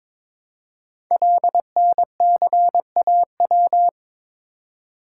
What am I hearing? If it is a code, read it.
Morse code, heard as a keyed tone, about 22 wpm, 700 Hz. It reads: LNCAW
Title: Code Groups - 5 Letters - 22wpm